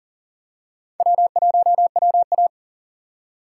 Morse code, heard as a keyed tone, about 40 wpm, 700 Hz. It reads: W1WA
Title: USA Call Signs - 4 Characters - 6x Character Spacing - 40wpm